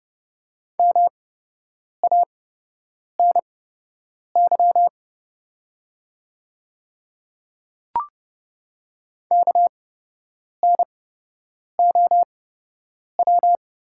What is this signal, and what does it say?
Morse code, heard as a keyed tone, about 31 wpm, spaced as for 8 wpm, 700 Hz. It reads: MANY KNOW
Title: 8x Character Spacing - Top 100 Words - 30wpm